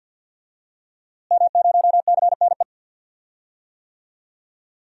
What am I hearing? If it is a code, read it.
Morse code, heard as a keyed tone, about 52 wpm, 700 Hz. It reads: M0CNE
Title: DXCC-100 Call Signs - 5 Characters - Repeated 3x - 50wpm